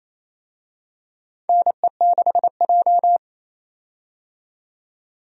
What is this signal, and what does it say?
Morse code, heard as a keyed tone, about 28 wpm, 700 Hz. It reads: NE6J
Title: USA Call Signs - 4 Characters - 28wpm